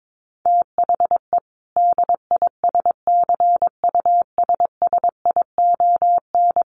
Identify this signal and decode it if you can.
Morse code, heard as a keyed tone, about 22 wpm, 700 Hz. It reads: THE DISCUSSION
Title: Sentences from Top 1100 Words 22wpm